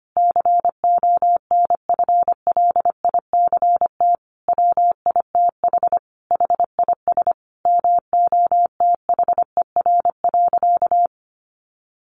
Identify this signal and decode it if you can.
Morse code, heard as a keyed tone, about 25 wpm, 700 Hz. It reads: CONFLICT WITH HIS MOTHER.
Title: Sentences from Top 1300 Words 25wpm